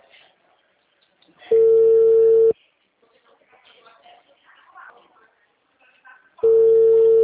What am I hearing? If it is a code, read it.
Ringback tone (China, 450 Hz).